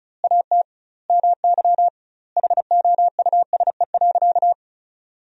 Morse code, heard as a keyed tone, about 34 wpm, 700 Hz. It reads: AT MY HOUSE.
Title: 2x Word-Spacing - Sentences from Top 200 Words - 35wpm